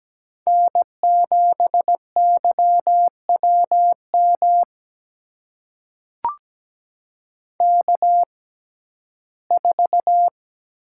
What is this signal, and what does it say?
Morse code, heard as a keyed tone, about 17 wpm, spaced as for 12 wpm, 700 Hz. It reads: N7YWM K4
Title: USA Call Signs - 5 Characters - 6x Character Spacing - 17wpm